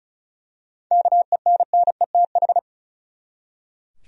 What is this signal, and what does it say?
Morse code, heard as a keyed tone, about 34 wpm, 700 Hz. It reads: KENNETH
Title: USA Common Names - 35wpm